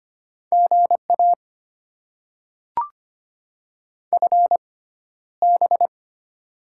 Morse code, heard as a keyed tone, about 25 wpm, spaced as for 13 wpm, 700 Hz. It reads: GA FB
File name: Top 20 QSO Elements - 6x Character Spacing - 25wpm